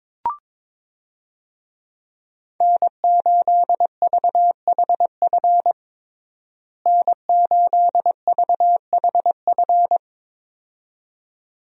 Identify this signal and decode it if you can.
Morse code, heard as a keyed tone, about 22 wpm, 700 Hz. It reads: N8VHF N8VHF
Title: USA Call Signs - 5 Characters - Repeated 2x - 22wpm